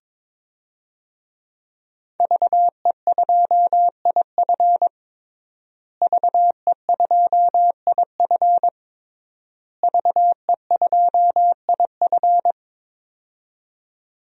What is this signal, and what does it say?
Morse code, heard as a keyed tone, about 22 wpm, 700 Hz. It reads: VE2IF VE2IF VE2IF
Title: DXCC-100 Call Signs - 5 Characters - Repeated 3x - 22wpm